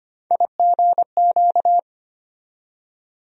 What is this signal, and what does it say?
Morse code, heard as a keyed tone, about 25 wpm, 700 Hz. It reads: IGQ
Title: USA Call Sign Postfixes - 3 Letters 25wpm